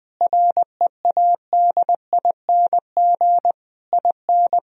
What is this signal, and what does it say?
Morse code, heard as a keyed tone, about 20 wpm, 700 Hz. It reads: READING IN
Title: Sentences from Top 1000 Words - Review Part 4 - 20wpm